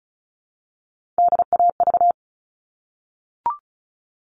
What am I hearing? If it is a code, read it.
Morse code, heard as a keyed tone, about 34 wpm, 700 Hz. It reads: DAV
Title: All Cut Numbers - 3-Digit 35wpm